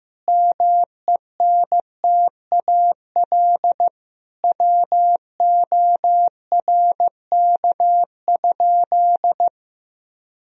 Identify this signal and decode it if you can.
Morse code, heard as a keyed tone, about 15 wpm, 700 Hz. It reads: MENTAL WORK?
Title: Sentences from Top 1000 Words - Review Part 4 - 15wpm